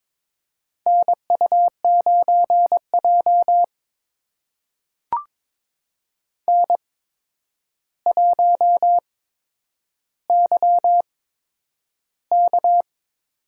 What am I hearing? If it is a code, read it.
Morse code, heard as a keyed tone, about 22 wpm, spaced as for 9 wpm, 700 Hz. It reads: NU9J N1YK
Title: USA Call Signs - 4 Characters - 8x Character Spacing - 22wpm